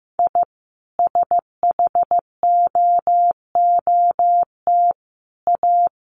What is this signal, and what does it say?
Morse code, heard as a keyed tone, about 15 wpm, 700 Hz. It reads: I SHOOT A
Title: A Sound Of Thunder 15wpm